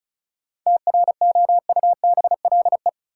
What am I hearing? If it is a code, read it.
Morse code, heard as a keyed tone, about 36 wpm, 700 Hz. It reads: TROUBLE
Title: Sentences from Top 1200 Words - New Word Drill Alt - 35wpm